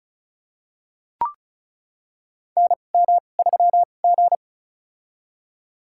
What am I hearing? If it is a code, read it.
Morse code, heard as a keyed tone, about 36 wpm, spaced as for 25 wpm, 700 Hz. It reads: NM3G
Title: USA Call Signs - 4 Characters - 2x Character Spacing - 35wpm